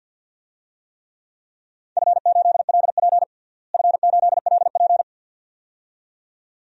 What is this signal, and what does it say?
Morse code, heard as a keyed tone, about 46 wpm, 700 Hz. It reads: F8LP F8LP
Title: DXCC-100 Call Signs - 4 Characters - Repeated 2x - 50wpm